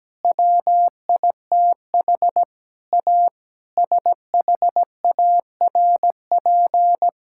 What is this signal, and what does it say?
Morse code, heard as a keyed tone, about 17 wpm, 700 Hz. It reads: WITH A SHARP
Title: Sentences from Top 1600 Words 17wpm